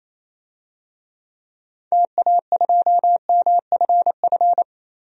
Morse code, heard as a keyed tone, about 28 wpm, 700 Hz. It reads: TA2MFF
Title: DXCC-100 Call Signs - 6 Characters - 28wpm